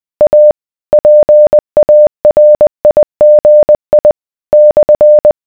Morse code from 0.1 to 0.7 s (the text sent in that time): A